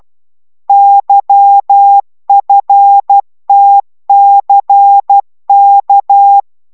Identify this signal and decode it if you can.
Morse code, heard as a keyed tone, about 12 wpm, 800 Hz. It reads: YFTCK
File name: taviro tanulas / cw 69 19 10